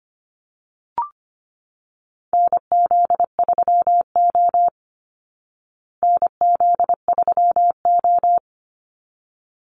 Morse code, heard as a keyed tone, about 25 wpm, 700 Hz. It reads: NZ3O NZ3O
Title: Call Signs Repeated 3x - 25wpm